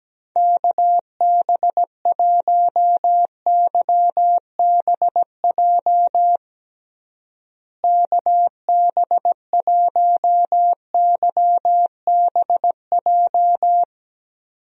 Morse code, heard as a keyed tone, about 17 wpm, 700 Hz. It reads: KB1YBJ KB1YBJ
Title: USA Call Signs - 6 Characters - Repeated 2x - 17wpm